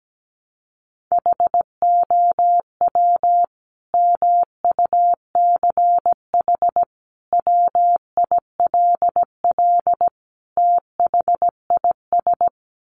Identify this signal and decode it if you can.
Morse code, heard as a keyed tone, about 17 wpm, 700 Hz. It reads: HOW MUCH WILL THIS
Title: Sentences from Top 1400 Words 17wpm